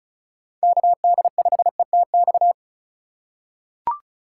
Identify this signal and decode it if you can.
Morse code, heard as a keyed tone, about 36 wpm, 700 Hz. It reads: KD5ETX
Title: USA Call Signs - 6 Characters - 3x Character Spacing - 35wpm